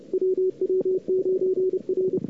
Morse code, heard as a keyed tone, about 30 wpm, 410 Hz. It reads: WW9R
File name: cw-sprint-k5tr-op-n5rz